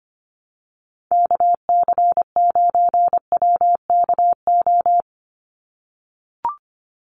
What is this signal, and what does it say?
Morse code, heard as a keyed tone, about 25 wpm, 700 Hz. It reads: KC9WKO
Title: USA Call Signs 25wpm